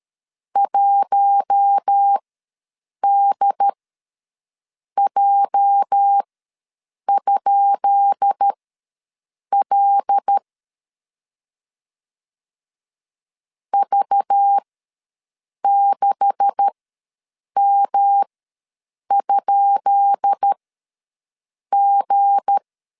Morse code, heard as a keyed tone, about 13 wpm, spaced as for 6 wpm, 800 Hz. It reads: 1DJ?L V6M?G
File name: LEZIONI / 10 SIGNS / 10B INT ws 12wpm 3527